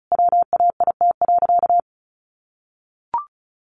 Morse code, heard as a keyed tone, about 34 wpm, 700 Hz. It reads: WAIT.
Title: Sentences from Top 800 Words 35wpm